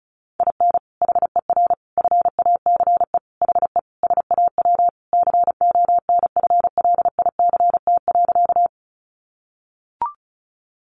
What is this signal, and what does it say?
Morse code, heard as a keyed tone, about 34 wpm, 700 Hz. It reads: IN HER FACE HE SAW CONFLICT.
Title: Sentences from Top 1300 Words 35wpm